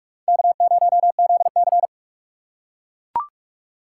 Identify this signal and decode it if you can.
Morse code, heard as a keyed tone, about 41 wpm, 700 Hz. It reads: K0ZC
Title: USA Call Signs - 4 Characters - 2x Character Spacing - 45wpm